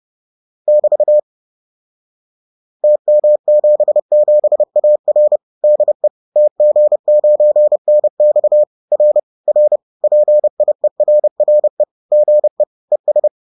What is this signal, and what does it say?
Morse code, heard as a keyed tone, about 30 wpm, 600 Hz. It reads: <BT> TM77AR DE TG9NX R R PIERRE GE ES